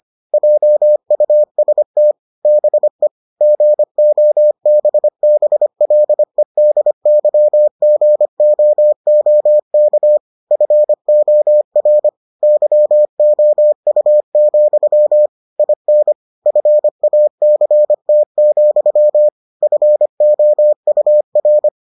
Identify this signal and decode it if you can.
Morse code, heard as a keyed tone, about 25 wpm, 600 Hz. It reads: JUST BE GOBBLEDYGOOK FOR YOU, IN FACT, FOUR